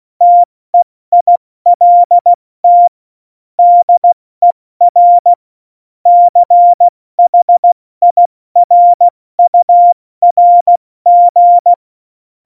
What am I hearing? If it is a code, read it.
Morse code, heard as a keyed tone, about 16 wpm, 700 Hz. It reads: TEILT DER CHIRURG
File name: Morsekurs / Witze / Witz 29. September 2020, 70 F80